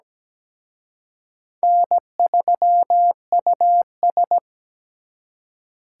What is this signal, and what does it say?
Morse code, heard as a keyed tone, about 20 wpm, 700 Hz. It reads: N3US
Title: Speed Racing - USA 1x2 Call Signs 17wpm